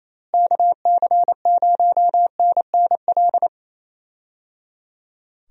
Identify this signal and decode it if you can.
Morse code, heard as a keyed tone, about 28 wpm, 700 Hz. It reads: KC0NNL
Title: USA Call Signs - 6 Characters - Repeated 3x - 28wpm